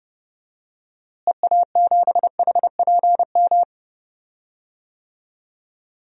Morse code, heard as a keyed tone, about 30 wpm, 700 Hz. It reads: EA7HPM
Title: DXCC-100 Call Signs - 6 Characters - Repeated 3x - 30wpm